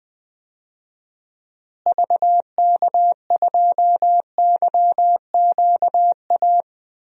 Morse code, heard as a keyed tone, about 20 wpm, 700 Hz. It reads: VK2YQA
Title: DXCC-100 Call Signs - 6 Characters - Repeated 2x - 20wpm